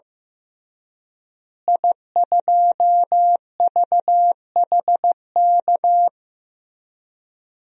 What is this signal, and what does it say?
Morse code, heard as a keyed tone, about 15 wpm, 700 Hz. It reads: I2VHK
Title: DXCC-100 Call Signs - 5 Characters - Repeated 3x - 15wpm